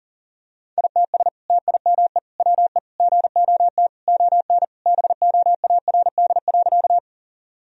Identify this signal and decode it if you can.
Morse code, heard as a keyed tone, about 40 wpm, 700 Hz. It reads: ITS TIME WE GOT ON BOARD.
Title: Sentences from Top 1000 Words - Review Part 4 - 40wpm